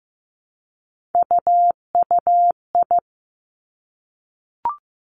Morse code, heard as a keyed tone, about 15 wpm, 700 Hz. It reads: UUI
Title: Three Challenging Letter-Number Combinations 15wpm